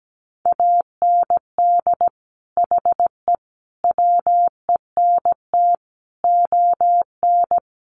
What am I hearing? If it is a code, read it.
Morse code, heard as a keyed tone, about 17 wpm, 700 Hz. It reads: AND HE WENT ON
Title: Sentences from Top 800 Words 17wpm